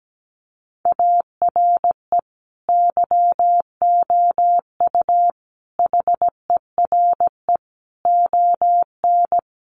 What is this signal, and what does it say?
Morse code, heard as a keyed tone, about 17 wpm, 700 Hz. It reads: ARE YOU HERE ON
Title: Sentences from Top 400 Words 17wpm